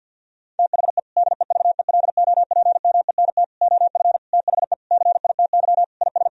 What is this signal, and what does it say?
Morse code, heard as a keyed tone, about 50 wpm, 700 Hz. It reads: THE DEVELOPMENT OF THE CITY IS
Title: Sentences from Top 1000 Words - Review Part 4 - 50wpm